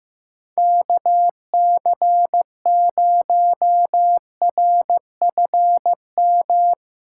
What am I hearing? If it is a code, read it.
Morse code, heard as a keyed tone, about 17 wpm, 700 Hz. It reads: KC0RFM
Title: Speed Racing - USA 2x3 Call Signs 15wpm